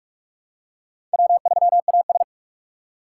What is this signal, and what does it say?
Morse code, heard as a keyed tone, about 46 wpm, 700 Hz. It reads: W3AS
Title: USA Call Signs - 4 Characters - 3x Character Spacing - 45wpm